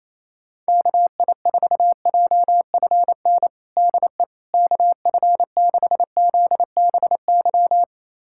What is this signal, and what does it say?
Morse code, heard as a keyed tone, about 28 wpm, 700 Hz. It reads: KI4JFN DE KF6ZBY
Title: USA Callsign DE Callsign - 6 Characters - 28wpm